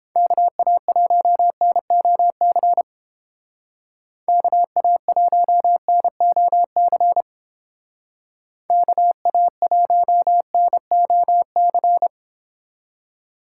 Morse code, heard as a keyed tone, about 31 wpm, 700 Hz. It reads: KA1NOC KA1NOC KA1NOC